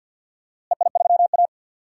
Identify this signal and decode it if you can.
Morse code, heard as a keyed tone, about 51 wpm, 700 Hz. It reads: EI3A
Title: DXCC-100 Call Signs - 4 Characters - Repeated 2x - 50wpm